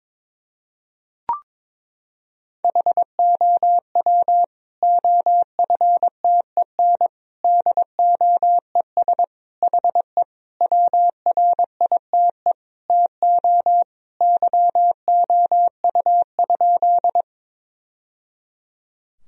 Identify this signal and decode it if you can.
Morse code, heard as a keyed tone, about 22 wpm, 700 Hz. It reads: HOW OFTEN DOES HE WRITE TO YOU?
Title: Sentences from Top 1000 Words - Review Part 1 - 22wpm